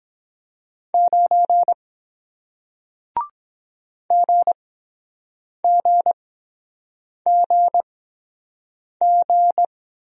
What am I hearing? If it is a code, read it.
Morse code, heard as a keyed tone, about 22 wpm, spaced as for 7 wpm, 700 Hz. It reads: 9 GGGG